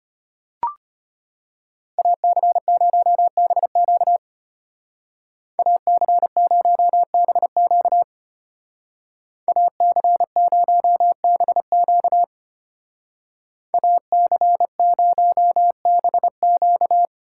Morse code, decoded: AC0BQ AC0BQ AC0BQ AC0BQ

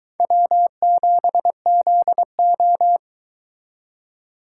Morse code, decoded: W7ZO

